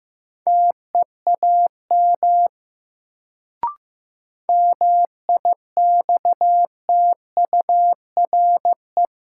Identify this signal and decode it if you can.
Morse code, heard as a keyed tone, about 15 wpm, 700 Hz. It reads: TEAM MIXTURE
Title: Sentences from Top 1500 Words - New Word Drill - 15wpm